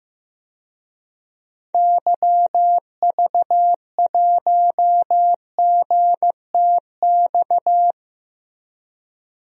Morse code, heard as a keyed tone, about 16 wpm, 700 Hz. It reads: YV1GTX
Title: DXCC-100 Call Signs - 6 Characters - Repeated 2x - 15wpm